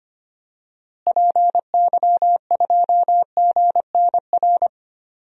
Morse code, heard as a keyed tone, about 25 wpm, 700 Hz. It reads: PY2GNR
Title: DXCC-100 Call Signs - 6 Characters - Repeated 2x - 25wpm